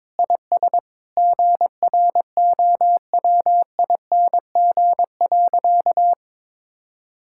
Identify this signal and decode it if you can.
Morse code, heard as a keyed tone, about 22 wpm, 700 Hz. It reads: IS GROWING.
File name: practice / 22wpm / Sentences with 8-Word Phrases - 22wpm